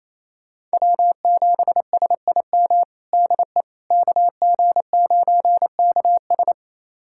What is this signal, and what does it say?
Morse code, heard as a keyed tone, about 28 wpm, 700 Hz. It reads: W7SIM DE KG9KS
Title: USA Callsign DE Callsign - 5 Characters - 28wpm